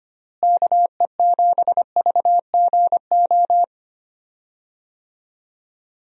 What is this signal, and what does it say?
Morse code, heard as a keyed tone, about 25 wpm, 700 Hz. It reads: KE7VGO
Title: USA Call Signs - 6 Characters - Repeated 3x - 25wpm